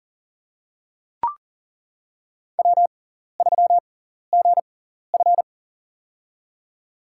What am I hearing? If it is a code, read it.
Morse code, heard as a keyed tone, about 40 wpm, spaced as for 14 wpm, 700 Hz. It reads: W3GF
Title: USA Call Signs - 4 Characters - 6x Character Spacing - 40wpm